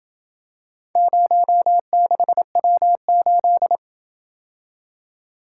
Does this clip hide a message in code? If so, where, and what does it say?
Yes: Morse code from 1.0 to 3.8 s, heard as a keyed tone, about 27 wpm, 700 Hz. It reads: 06W8